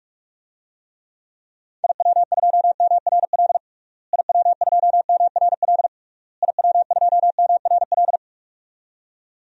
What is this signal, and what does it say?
Morse code, heard as a keyed tone, about 46 wpm, 700 Hz. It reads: IW2MRL IW2MRL IW2MRL
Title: DXCC-100 Call Signs - 6 Characters - Repeated 3x - 45wpm